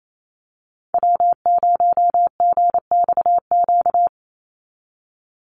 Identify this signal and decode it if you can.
Morse code, heard as a keyed tone, about 28 wpm, 700 Hz. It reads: W0GXQ
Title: Call Signs Repeated 3x - 28wpm